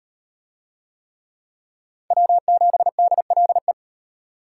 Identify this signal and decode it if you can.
Morse code, heard as a keyed tone, about 38 wpm, 700 Hz. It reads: W7DLE